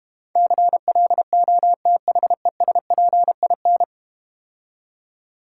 Morse code, heard as a keyed tone, about 32 wpm, 700 Hz. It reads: CLOTHESPIN